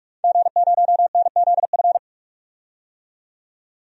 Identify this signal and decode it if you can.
Morse code, heard as a keyed tone, about 41 wpm, 700 Hz. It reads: G0NZF
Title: DXCC-100 Call Signs - 5 Characters - 45wpm